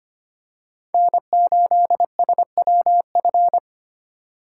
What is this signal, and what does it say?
Morse code, heard as a keyed tone, about 25 wpm, 690 Hz. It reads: N8SWF
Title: USA Call Signs - 5 Characters - 6x Character Spacing - 25wpm